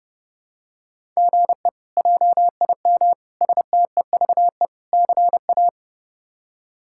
Morse code, heard as a keyed tone, about 32 wpm, 700 Hz. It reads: GE JIM STEVE CA
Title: K1USN SST QSO Practice - 30wpm